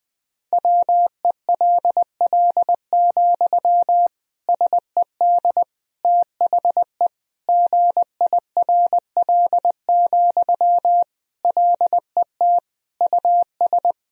Morse code, heard as a keyed tone, about 20 wpm, 700 Hz. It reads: WELL, SED THE GIRL, LET US
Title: Sentences from Top 400 Words and Abbreviations 20wpm